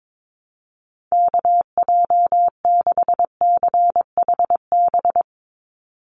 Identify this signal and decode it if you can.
Morse code, heard as a keyed tone, about 22 wpm, 700 Hz. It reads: KJ6CHB
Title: USA Call Signs 22wpm